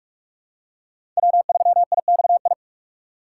Morse code, heard as a keyed tone, about 44 wpm, 700 Hz. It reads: W3IXI
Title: USA Call Signs - 5 Characters - 6x Character Spacing - 45wpm